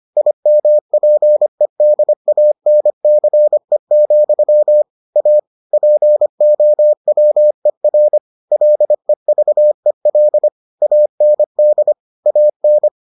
Morse code, heard as a keyed tone, about 25 wpm, 600 Hz. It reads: IMPEDANCE, A POWER LEVEL AND AN